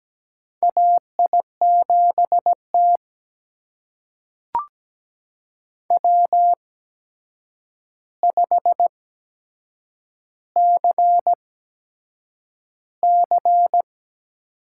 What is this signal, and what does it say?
Morse code, heard as a keyed tone, about 17 wpm, spaced as for 7 wpm, 700 Hz. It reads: AI7T W5CC
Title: USA Call Signs - 4 Characters - 8x Character Spacing - 17wpm